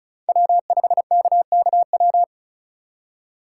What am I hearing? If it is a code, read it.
Morse code, heard as a keyed tone, about 46 wpm, 700 Hz. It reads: W5KKW